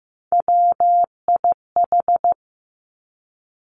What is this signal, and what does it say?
Morse code, heard as a keyed tone, about 15 wpm, 700 Hz. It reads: WIH
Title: Three Challenging Letter-Number Combinations 15wpm